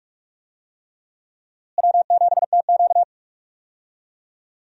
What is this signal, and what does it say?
Morse code, heard as a keyed tone, about 46 wpm, 700 Hz. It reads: W7TQ
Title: USA Call Signs - 4 Characters - Repeated 3x - 45wpm